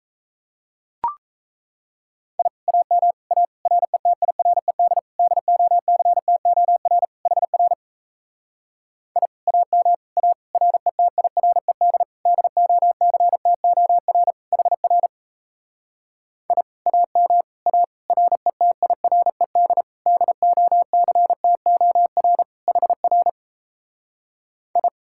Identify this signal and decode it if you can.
Morse code, heard as a keyed tone, about 37 wpm, 700 Hz. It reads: I AM A RETIRED DOCTOR HR I AM A RETIRED DOCTOR HR I AM A RETIRED DOCTOR HR I